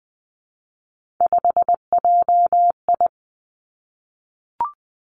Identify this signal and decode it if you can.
Morse code, heard as a keyed tone, about 20 wpm, 700 Hz. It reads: 5JI
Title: Three Challenging Letter-Number Combinations 20wpm